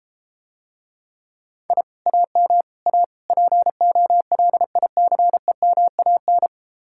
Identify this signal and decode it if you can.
Morse code, heard as a keyed tone, about 34 wpm, 700 Hz. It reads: I AM A POLICEMAN